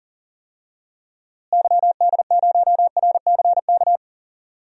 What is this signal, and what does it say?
Morse code, heard as a keyed tone, about 40 wpm, 700 Hz. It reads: YD0RCK
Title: DXCC-100 Call Signs - 6 Characters - Repeated 3x - 40wpm